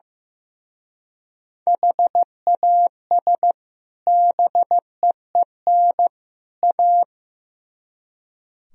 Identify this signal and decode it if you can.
Morse code, heard as a keyed tone, about 16 wpm, 700 Hz. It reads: HAS BEEN A
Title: Speed-Racing - Sets of 3 Words 15wpm